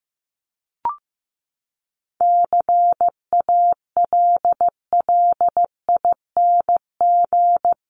Morse code, heard as a keyed tone, about 15 wpm, 700 Hz. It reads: CALLING
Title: ING Suffix - Top 100 Words with Base Words 15wpm